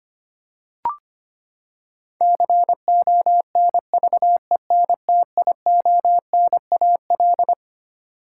Morse code, heard as a keyed tone, about 25 wpm, 700 Hz. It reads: CONVENTIONAL
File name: Twelve Letter Words - 25wpm